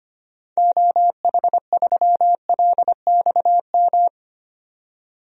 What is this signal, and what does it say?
Morse code, heard as a keyed tone, about 25 wpm, 700 Hz. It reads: OH3LXM
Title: DXCC-100 Call Signs - 6 Characters - 25wpm